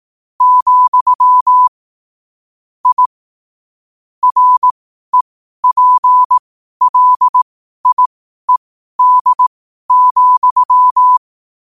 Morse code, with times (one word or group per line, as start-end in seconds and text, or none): 0.4-1.7 ,
2.9-3.1 I
4.2-11.2 REPLIED,